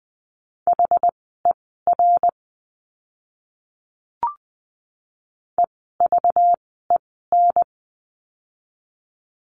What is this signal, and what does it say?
Morse code, heard as a keyed tone, about 20 wpm, spaced as for 15 wpm, 700 Hz. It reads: HER EVEN
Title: 206 Lesson 33f - Letter j - Words 20wpm